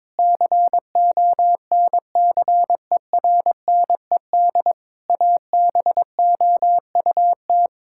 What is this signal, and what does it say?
Morse code, heard as a keyed tone, about 22 wpm, 700 Hz. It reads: CONCERNED ABOUT
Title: Sentences from Top 1000 Words - Review Part 6 - 22wpm